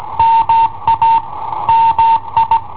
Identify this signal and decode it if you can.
Morse code, heard as a keyed tone, about 19 wpm, 910 Hz. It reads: MA MI
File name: i7ohp / cassetta i7ohp B